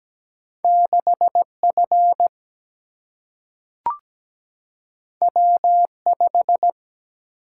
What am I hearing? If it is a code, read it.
Morse code, heard as a keyed tone, about 17 wpm, 700 Hz. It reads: 6F W5
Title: Single-Letter Single-Number 17wpm